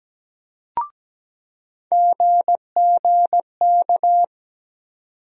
Morse code, heard as a keyed tone, about 17 wpm, 700 Hz. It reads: GGK
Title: USA Call Sign Postfixes - 3 Letters 17wpm